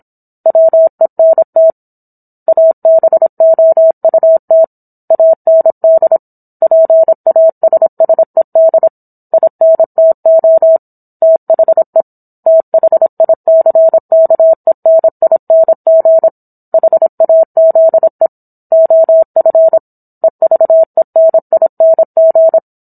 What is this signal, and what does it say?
Morse code, heard as a keyed tone, about 27 wpm, 650 Hz. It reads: WENT ABOUT AND PASSED INTO THE THICKENING HAZE OF EVENING